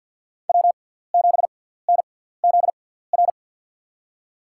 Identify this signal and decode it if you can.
Morse code, heard as a keyed tone, about 46 wpm, spaced as for 17 wpm, 700 Hz. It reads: W7NZR
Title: USA Call Signs - 5 Characters - 6x Character Spacing - 50wpm